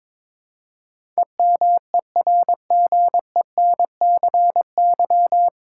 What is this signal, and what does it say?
Morse code, heard as a keyed tone, about 22 wpm, 700 Hz. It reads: EMERGENCY
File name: practice / 22wpm / Nine Letter Words - 22wpm